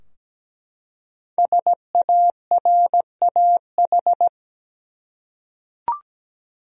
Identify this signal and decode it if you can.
Morse code, heard as a keyed tone, about 17 wpm, 700 Hz. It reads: SARAH
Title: USA Common Names - Max 5 Letters - 17wpm